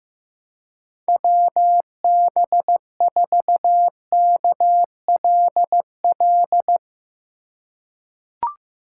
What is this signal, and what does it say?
Morse code, heard as a keyed tone, about 15 wpm, 700 Hz. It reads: WB4KLL
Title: USA Call Signs - 6 Characters - 3x Character Spacing - 15wpm